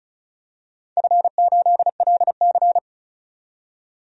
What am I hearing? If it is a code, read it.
Morse code, heard as a keyed tone, about 35 wpm, 700 Hz. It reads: F8LC
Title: DXCC-100 Call Signs - 4 Characters - 35wpm